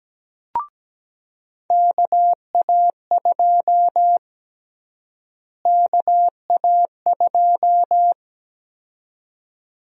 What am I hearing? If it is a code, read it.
Morse code, heard as a keyed tone, about 17 wpm, 700 Hz. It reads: KA2 KA2
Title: USA Call Sign Prefixes - Repeated 2x - 17wpm